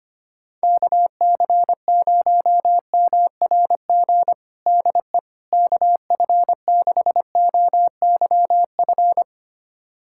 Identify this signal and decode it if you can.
Morse code, heard as a keyed tone, about 25 wpm, 700 Hz. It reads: KC0MRG DE KF6OYF
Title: USA Callsign DE Callsign - 6 Characters - 25wpm